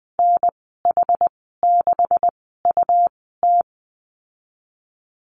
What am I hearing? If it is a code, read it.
Morse code, heard as a keyed tone, about 20 wpm, spaced as for 15 wpm, 700 Hz. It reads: NH6UT
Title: 147 Lesson 24g - Letter y - Call Signs 20wpm